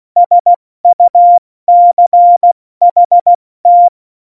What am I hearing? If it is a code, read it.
Morse code, heard as a keyed tone, about 16 wpm, 700 Hz. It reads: SUCHT
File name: Morsekurs / Witze / Witz 12. Januar 2021, 70 F80